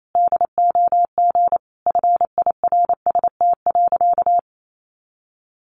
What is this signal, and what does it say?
Morse code, heard as a keyed tone, about 28 wpm, 700 Hz. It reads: DOG FIRST.
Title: Sentences from Top 400 Words 28wpm